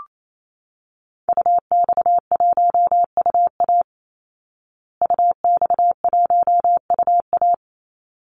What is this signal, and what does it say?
Morse code, heard as a keyed tone, about 28 wpm, 700 Hz. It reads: UX1UA UX1UA
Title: Call Signs Repeated 3x - 28wpm